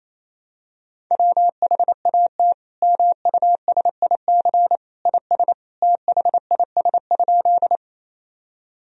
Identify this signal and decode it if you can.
Morse code, heard as a keyed tone, about 28 wpm, 700 Hz. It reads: WHAT MUSIC IS THIS?
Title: Sentences from Top 1000 Words - Review Part 1 - 28wpm